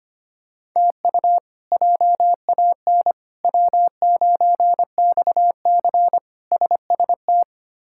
Morse code, heard as a keyed tone, about 25 wpm, 700 Hz. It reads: TU JAN W9XC SST
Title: K1USN SST QSO Practice - 25wpm